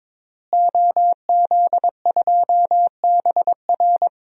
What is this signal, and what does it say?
Morse code, heard as a keyed tone, about 22 wpm, 700 Hz. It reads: OZ2BR
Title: DXCC-100 Call Signs - 5 Characters - Repeated 3x - 22wpm